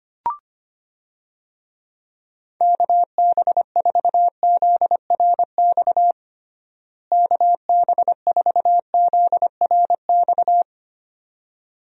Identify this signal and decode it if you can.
Morse code, heard as a keyed tone, about 25 wpm, 700 Hz. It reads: KB4ZRX KB4ZRX
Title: USA Call Signs - 6 Characters - Repeated 2x - 25wpm